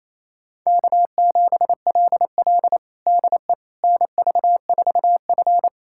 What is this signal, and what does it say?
Morse code, heard as a keyed tone, about 27 wpm, 700 Hz. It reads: K7LL DE NV4F
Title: USA Callsign DE Callsign - 4 Characters - 28wpm